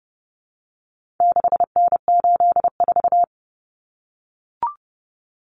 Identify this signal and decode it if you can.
Morse code, heard as a keyed tone, about 30 wpm, 700 Hz. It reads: 6N84
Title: Common Cut Numbers - 4-Digit 30wpm